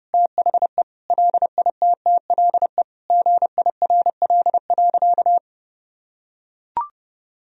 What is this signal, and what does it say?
Morse code, heard as a keyed tone, about 30 wpm, 700 Hz. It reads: THE LITTLE GIRL.
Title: Sentences from Top 400 Words and Abbreviations 30wpm